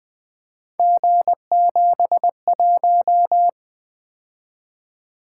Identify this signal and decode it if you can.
Morse code, heard as a keyed tone, about 20 wpm, 700 Hz. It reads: G71
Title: Three Letter-Number Combinations - 20wpm